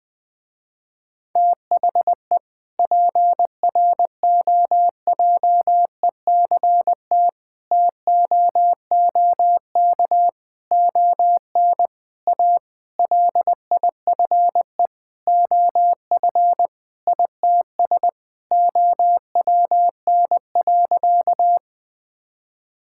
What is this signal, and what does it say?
Morse code, heard as a keyed tone, about 20 wpm, 700 Hz. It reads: THE PROJECT TOOK ON A LIFE OF ITS OWN.
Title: Sentences with 7-Word Phrases - Spoken Beginning - 20wpm